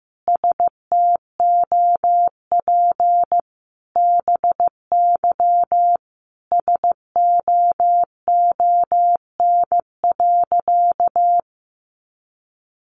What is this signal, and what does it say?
Morse code, heard as a keyed tone, about 15 wpm, 700 Hz. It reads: STOP BY SOON.
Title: Sentences from Top 1500 Words 15wpm